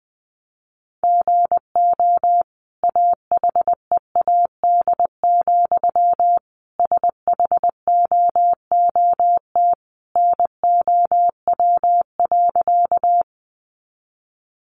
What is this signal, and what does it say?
Morse code, heard as a keyed tone, about 20 wpm, 700 Hz. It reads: GO AHEAD, SHOOT NOW.
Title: Sentences from Top 1200 Words 20wpm